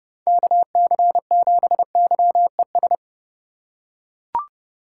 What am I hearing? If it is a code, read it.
Morse code, heard as a keyed tone, about 31 wpm, 700 Hz. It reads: KC7YES
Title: USA Call Signs - 2x Character Spacing - 30wpm